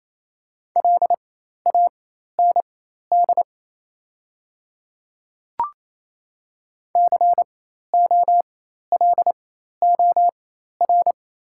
Morse code, heard as a keyed tone, about 28 wpm, spaced as for 13 wpm, 700 Hz. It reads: LAND COLOR
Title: 4x Character Spacing - Top 300 Words - 28wpm